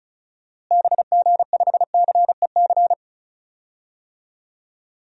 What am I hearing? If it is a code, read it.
Morse code, heard as a keyed tone, about 36 wpm, 700 Hz. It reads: BG5CEC
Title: DXCC-100 Call Signs - 6 Characters - Repeated 3x - 35wpm